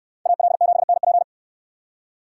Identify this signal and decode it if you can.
Morse code, heard as a keyed tone, about 64 wpm, 700 Hz. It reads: AL7NP